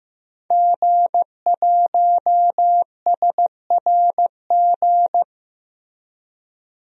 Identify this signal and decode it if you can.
Morse code, heard as a keyed tone, about 16 wpm, 700 Hz. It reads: G1SRG
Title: DXCC-100 Call Signs - 5 Characters - Repeated 2x - 15wpm